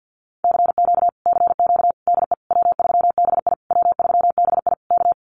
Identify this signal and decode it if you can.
Morse code, heard as a keyed tone, about 50 wpm, 700 Hz. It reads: CQ CQ DE W3BI W3BI K
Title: Common QSO Phrases Encore 50wpm